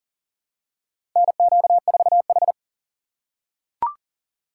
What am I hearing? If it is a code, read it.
Morse code, heard as a keyed tone, about 37 wpm, 700 Hz. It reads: NQ4H
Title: USA Call Signs - 4 Characters - 4x Character Spacing - 40wpm